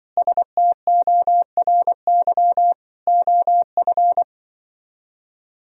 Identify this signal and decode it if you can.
Morse code, heard as a keyed tone, about 24 wpm, 700 Hz. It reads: STORY OF
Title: Speed-Racing - Sets of 3 Words 20wpm